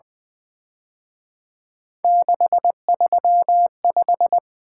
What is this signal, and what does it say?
Morse code, heard as a keyed tone, about 24 wpm, 700 Hz. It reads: 635
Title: Speed-Racing - Three Letter-Number Combinations - 20wpm